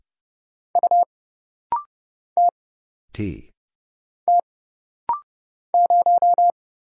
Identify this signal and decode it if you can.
Morse code, heard as a keyed tone, about 30 wpm, 700 Hz. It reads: U T T 0
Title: Single Letter-Number - Rapid-Fire 30wpm